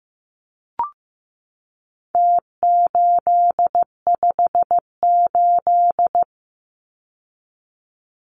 Morse code, heard as a keyed tone, about 15 wpm, 700 Hz. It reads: T858
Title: Numbers - Contest Style - 4-Digit 15wpm